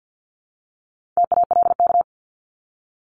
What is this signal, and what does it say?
Morse code, heard as a keyed tone, about 50 wpm, 700 Hz. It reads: TALK
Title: Multi-letter Phonemes 50wpm